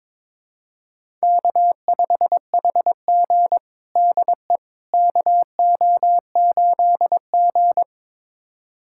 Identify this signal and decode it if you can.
Morse code, heard as a keyed tone, about 22 wpm, 700 Hz. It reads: K5HG DE KO8G
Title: USA Callsign DE Callsign - 4 Characters - 22wpm